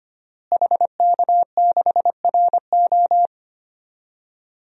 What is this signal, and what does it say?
Morse code, heard as a keyed tone, about 25 wpm, 700 Hz. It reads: HK6RO
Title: DXCC-100 Call Signs - 5 Characters - Repeated 3x - 25wpm